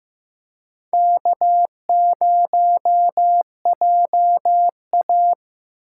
Morse code, heard as a keyed tone, about 15 wpm, 700 Hz. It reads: K0JA
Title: USA Call Signs - 4 Characters - 8x Character Spacing - 15wpm